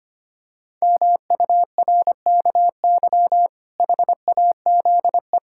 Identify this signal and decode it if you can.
Morse code, heard as a keyed tone, about 25 wpm, 700 Hz. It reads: MURKY HAZE
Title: Pangram Challenge v2 - 25wpm